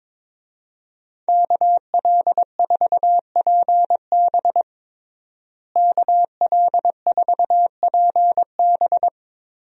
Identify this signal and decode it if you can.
Morse code, heard as a keyed tone, about 22 wpm, 700 Hz. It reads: KL4PB KL4PB
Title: USA Call Signs - 5 Characters - Repeated 3x - 22wpm